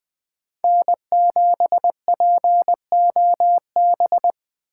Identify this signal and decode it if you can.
Morse code, heard as a keyed tone, about 20 wpm, 700 Hz. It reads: N7POB
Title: USA Call Signs - 5 Characters - 6x Character Spacing - 20wpm